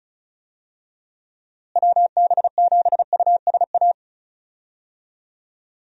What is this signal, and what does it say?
Morse code, heard as a keyed tone, about 36 wpm, 700 Hz. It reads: WB7USA
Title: USA Call Signs - 6 Characters - Repeated 3x - 35wpm